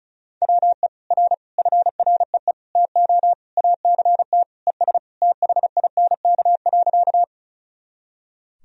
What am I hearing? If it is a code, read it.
Morse code, heard as a keyed tone, about 36 wpm, 700 Hz. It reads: WE R FREE TO ACT ES THINK.
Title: Sentences from Top 500 Words and Abbreviations 35wpm